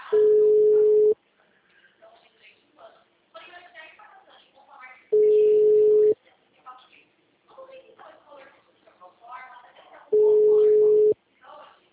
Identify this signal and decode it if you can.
Ringback tone (Europe, 425 Hz), with speech.
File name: out- -2005- - - .32722